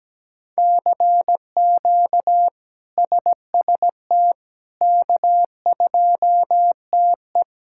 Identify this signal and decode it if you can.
Morse code, heard as a keyed tone, about 17 wpm, 700 Hz. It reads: CQ SST K2TE
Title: K1USN SST QSO Practice - 17wpm